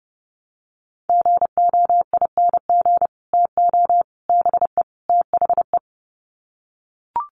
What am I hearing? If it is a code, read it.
Morse code, heard as a keyed tone, about 29 wpm, 700 Hz. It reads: GOING TO BE THE
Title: Sets of 4 Words 30wpm